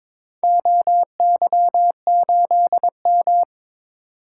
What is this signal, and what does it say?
Morse code, heard as a keyed tone, about 21 wpm, 700 Hz. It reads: OY8M
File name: practice / 22wpm / Four Letter-Number Combinations - 22wpm